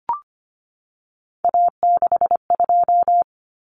Morse code, heard as a keyed tone, about 25 wpm, 700 Hz. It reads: A62
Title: Common Cut Numbers - 3-Digit 25wpm